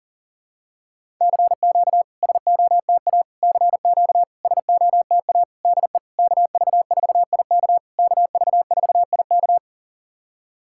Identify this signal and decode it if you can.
Morse code, heard as a keyed tone, about 44 wpm, 700 Hz. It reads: CQ SOTA CQ SOTA DE KV4IK KV4IK
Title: SOTA QSO Practice 40wpm